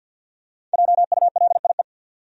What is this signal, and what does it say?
Morse code, heard as a keyed tone, about 52 wpm, 700 Hz. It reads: JULIE
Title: USA Common Names - Max 5 Letters - 50wpm